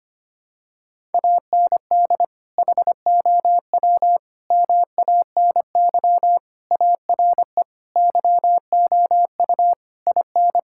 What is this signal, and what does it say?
Morse code, heard as a keyed tone, about 25 wpm, 700 Hz. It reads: AND HOW MANY ARE YOU IN
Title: Sentences from Top 1000 Words - Review Part 6 - 25wpm